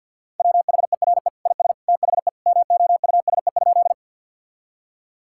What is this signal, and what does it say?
Morse code, heard as a keyed tone, about 52 wpm, 700 Hz. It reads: WHERE IS THE MOUSE?
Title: Sentences from Top 1000 Words - Review Part 5 - 50wpm